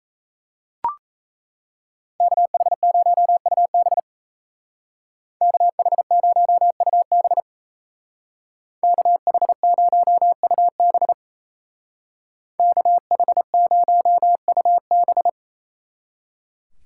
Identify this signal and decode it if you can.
Morse code, heard as a keyed tone, about 34 wpm, 700 Hz. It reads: KH0UB KH0UB KH0UB KH0UB
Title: Speed Racing - USA 2x2 Call Signs 28wpm